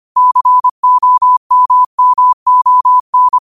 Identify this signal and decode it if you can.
Morse code, heard as a keyed tone, about 25 wpm, 1000 Hz. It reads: COMMON